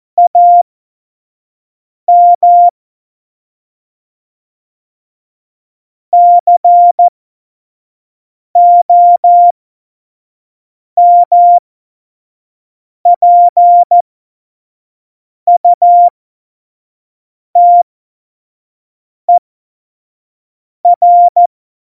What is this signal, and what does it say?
Morse code, heard as a keyed tone, about 14 wpm, spaced as for 5 wpm, 700 Hz. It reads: AM COMPUTER
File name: Morsekurs / Witze / Witz 04 Juli 17, Tempo 25 F70